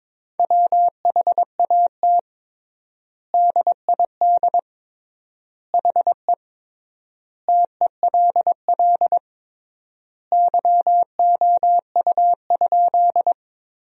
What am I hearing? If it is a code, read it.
Morse code, heard as a keyed tone, about 22 wpm, 700 Hz. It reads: WHAT DID HE TELL YOU?
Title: 3x Word-Spacing - Sentences from Top 200 Words - 22wpm